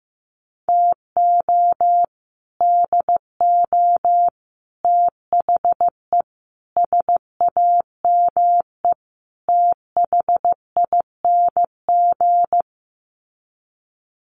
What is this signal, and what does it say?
Morse code, heard as a keyed tone, about 15 wpm, 700 Hz. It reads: TO DO THE SAME THING
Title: Sets of 5 Words Encore 15wpm